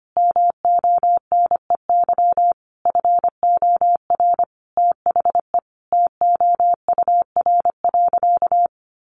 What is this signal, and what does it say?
Morse code, heard as a keyed tone, about 25 wpm, 700 Hz. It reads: MONEY FOR THE TOUR.
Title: Sentences from Top 1400 Words 25wpm